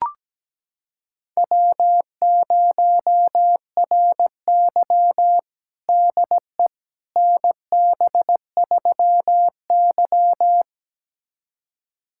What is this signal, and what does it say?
Morse code, heard as a keyed tone, about 17 wpm, 700 Hz. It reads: W0RY DE NB3Y
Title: USA Callsign DE Callsign - 4 Characters - 17wpm